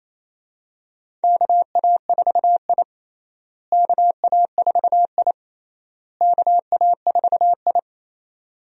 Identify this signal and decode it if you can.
Morse code, heard as a keyed tone, about 28 wpm, 700 Hz. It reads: KA4I KA4I KA4I
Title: USA Call Signs - 4 Characters - Repeated 3x - 28wpm